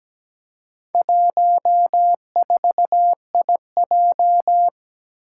Morse code, heard as a keyed tone, about 17 wpm, 700 Hz. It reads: 14IJ
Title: Four Challenging Letter-Number Combinations 17wpm